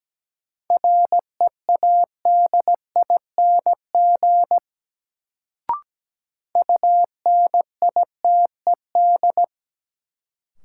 Morse code, heard as a keyed tone, about 17 wpm, 700 Hz. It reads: READING UNITED
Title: Sentences from Top 600 Words - New Word Drill - 17wpm